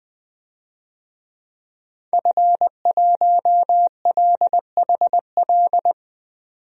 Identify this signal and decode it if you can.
Morse code, heard as a keyed tone, about 20 wpm, 700 Hz. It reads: F1LHL
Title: DXCC-100 Call Signs - 5 Characters - 20wpm